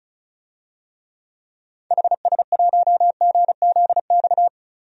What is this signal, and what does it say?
Morse code, heard as a keyed tone, about 36 wpm, 690 Hz. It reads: HS1GZX
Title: DXCC-100 Call Signs - 6 Characters - 35wpm